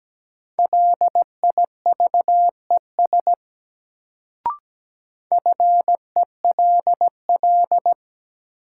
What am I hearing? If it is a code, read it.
Morse code, heard as a keyed tone, about 17 wpm, 700 Hz. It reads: LIVES FELL
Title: Sets of 5 Words Encore - New Word Drill - 17wpm